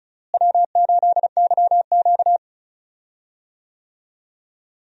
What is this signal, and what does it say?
Morse code, heard as a keyed tone, about 34 wpm, 700 Hz. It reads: W8YQ
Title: USA Call Signs - 4 Characters - Repeated 3x - 35wpm